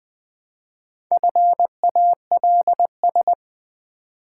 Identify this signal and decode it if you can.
Morse code, heard as a keyed tone, about 20 wpm, 700 Hz. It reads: FALS
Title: Build A Word - 1 Letter At A Time - 3 to 5-Letter Words - 20wpm